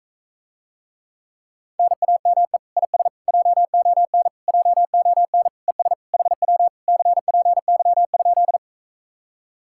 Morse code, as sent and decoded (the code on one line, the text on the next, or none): -. .- -- . / .. ... / .--- --- -. / .--- --- -. / . ... / .... .-- / -.-. .--. -.-- ..--..
NAME IS JON JON ES HW CPY?